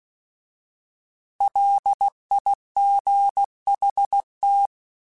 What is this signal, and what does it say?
Morse code, heard as a keyed tone, about 16 wpm, 800 Hz. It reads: LIGHT
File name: Alphabetical / Alphabetical mp3s / Ls @15